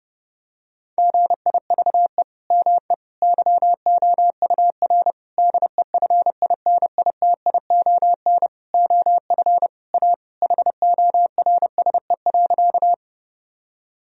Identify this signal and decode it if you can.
Morse code, heard as a keyed tone, about 30 wpm, 700 Hz. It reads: GIVE ME YOUR DEFINITION OF A HORSE.
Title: Sentences from Top 1000 Words - Review Part 4 - 30wpm